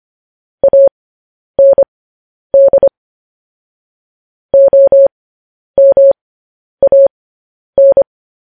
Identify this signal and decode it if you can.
Morse code, heard as a keyed tone, about 25 wpm, spaced as for 10 wpm, 550 Hz. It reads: AND OMAN
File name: podcast / News headlines-1x-25x10